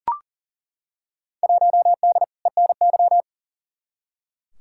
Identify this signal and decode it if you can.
Morse code, heard as a keyed tone, about 40 wpm, 700 Hz. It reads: 1D ENY
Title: Field Day Exchanges - 40wpm